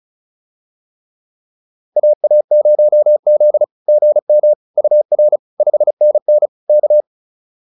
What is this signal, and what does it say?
Morse code, heard as a keyed tone, about 34 wpm, 600 Hz. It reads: AA0Z GM UR 5NN K